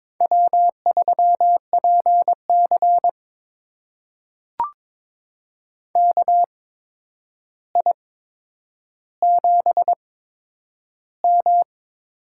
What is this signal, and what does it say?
Morse code, heard as a keyed tone, about 22 wpm, spaced as for 9 wpm, 700 Hz. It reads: W3PC KI7M
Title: USA Call Signs - 8x Character Spacing - 22wpm